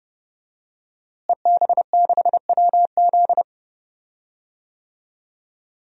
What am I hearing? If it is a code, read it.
Morse code, heard as a keyed tone, about 30 wpm, 700 Hz. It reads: EB6WZ
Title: DXCC-100 Call Signs - 5 Characters - Repeated 3x - 30wpm